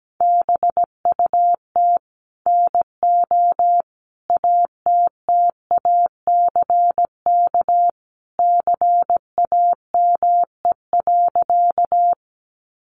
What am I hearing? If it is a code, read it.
Morse code, heard as a keyed tone, about 17 wpm, 700 Hz. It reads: BUT NO ATTACK CAME.
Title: Sentences from Top 1500 Words 17wpm